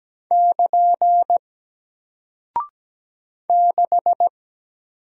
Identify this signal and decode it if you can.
Morse code, heard as a keyed tone, about 17 wpm, 700 Hz. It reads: <KN> 6
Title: Single Character (Letter:Number:Punctuation:Prosign) Alt 17wpm